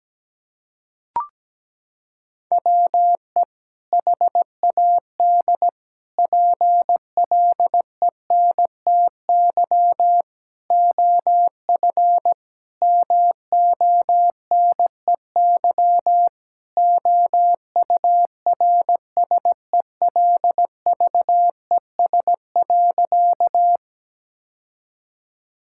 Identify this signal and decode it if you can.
Morse code, heard as a keyed tone, about 17 wpm, 700 Hz. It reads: WE HAD PLENTY OF MONEY OURSELVES.
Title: Sentences from Top 1600 Words 17wpm